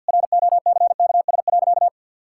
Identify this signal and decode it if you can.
Morse code, heard as a keyed tone, about 50 wpm, 700 Hz. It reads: ROCKS.
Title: Pangram Challenge v2 - 50wpm